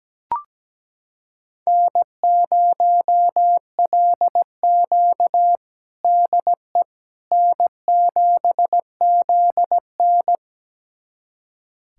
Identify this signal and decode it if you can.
Morse code, heard as a keyed tone, about 17 wpm, 700 Hz. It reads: N0LQ DE N7ZN
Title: USA Callsign DE Callsign - 4 Characters - 17wpm